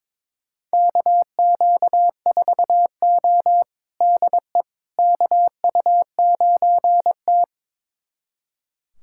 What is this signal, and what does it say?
Morse code, heard as a keyed tone, about 22 wpm, 700 Hz. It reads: KQ4O DE KU9T
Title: USA Callsign DE Callsign - 4 Characters - 22wpm